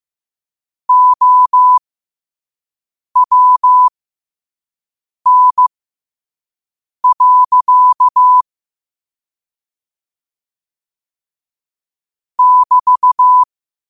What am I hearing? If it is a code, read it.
Morse code, heard as a keyed tone, about 15 wpm, spaced as for 5 wpm, 1000 Hz. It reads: OWN. <BT>
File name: qotdfiles / 5wpm